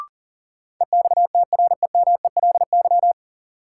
Morse code, heard as a keyed tone, about 40 wpm, 700 Hz. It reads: EXTREMELY
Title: Sentences from Top 1000 Words - Review Part 7 - New Word Drill - 40wpm